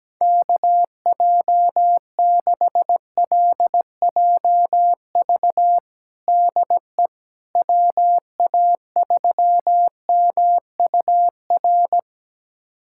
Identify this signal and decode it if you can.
Morse code, heard as a keyed tone, about 17 wpm, 700 Hz. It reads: KJ6LJV DE WA3MUR
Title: USA Callsign DE Callsign - 6 Characters - 17wpm